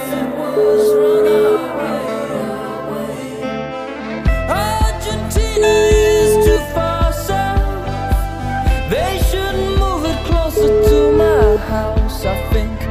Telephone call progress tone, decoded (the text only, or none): ringback tone